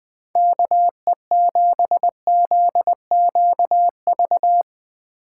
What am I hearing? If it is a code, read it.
Morse code, heard as a keyed tone, about 20 wpm, 700 Hz. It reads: KE7ZQV
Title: USA Call Signs - 6 Characters - 8x Character Spacing - 20wpm